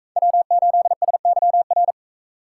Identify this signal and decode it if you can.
Morse code, heard as a keyed tone, about 40 wpm, 700 Hz. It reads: W8SYR